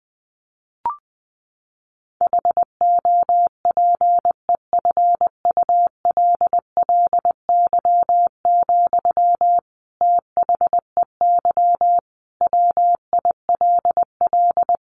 Morse code, heard as a keyed tone, about 20 wpm, 700 Hz. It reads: HOPEFULLY, THEY WILL
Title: Sentences from Top 1500 Words 20wpm